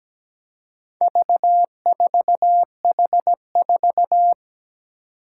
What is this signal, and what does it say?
Morse code, heard as a keyed tone, about 17 wpm, 700 Hz. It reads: V4H4
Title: Four Challenging Letter-Number Combinations 17wpm